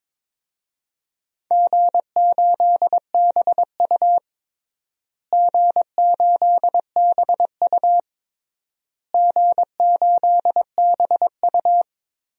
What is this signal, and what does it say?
Morse code, heard as a keyed tone, about 22 wpm, 700 Hz. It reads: G8BU G8BU G8BU
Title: DXCC-100 Call Signs - 4 Characters - Repeated 3x - 22wpm